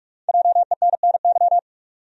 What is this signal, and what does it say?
Morse code, heard as a keyed tone, about 44 wpm, 700 Hz. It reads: JENNY
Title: USA Common Names - Max 5 Letters - 45wpm